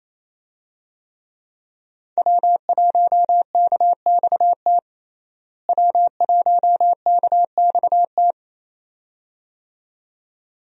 Morse code, heard as a keyed tone, about 28 wpm, 700 Hz. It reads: W1KXT W1KXT
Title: USA Call Signs - 5 Characters - Repeated 2x - 28wpm